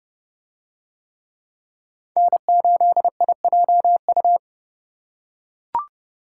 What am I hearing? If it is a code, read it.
Morse code, heard as a keyed tone, about 28 wpm, 700 Hz. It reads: N8IJU